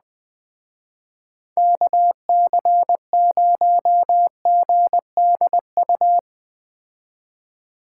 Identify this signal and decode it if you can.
Morse code, heard as a keyed tone, about 20 wpm, 700 Hz. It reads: KC0GDU
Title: Speed Racing - USA 2x3 Call Signs 17wpm